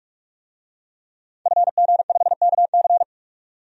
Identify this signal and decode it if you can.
Morse code, heard as a keyed tone, about 46 wpm, 700 Hz. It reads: FG5KC
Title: DXCC-100 Call Signs - 5 Characters - 45wpm